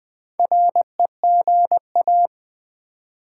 Morse code, heard as a keyed tone, about 20 wpm, 700 Hz. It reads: REGA
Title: Build A Word - 2 Letters At A Time - 6-Letter Words - 20wpm